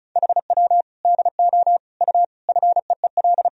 Morse code, heard as a keyed tone, about 36 wpm, 700 Hz. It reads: HW DO U FEEL
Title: Sentences from Top 400 Words and Abbreviations 35wpm